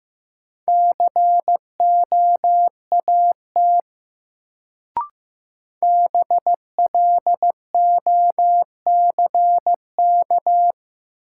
Morse code, heard as a keyed tone, about 16 wpm, 700 Hz. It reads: COAT BLOCK
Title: Sentences from Top 1600 Words - New Word Drill - 15wpm